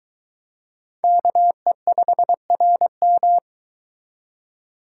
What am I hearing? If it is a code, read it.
Morse code, heard as a keyed tone, about 22 wpm, 700 Hz. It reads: KE5RM